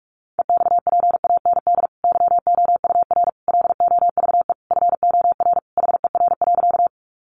Morse code, heard as a keyed tone, about 44 wpm, 700 Hz. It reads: EXPAND YOUR LOVE FOR HER.
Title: Sentences from Top 1400 Words 45wpm